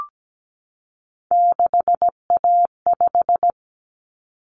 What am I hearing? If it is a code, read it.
Morse code, heard as a keyed tone, about 17 wpm, 700 Hz. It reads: 6A5
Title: Common Cut Numbers - 3-Digit 17wpm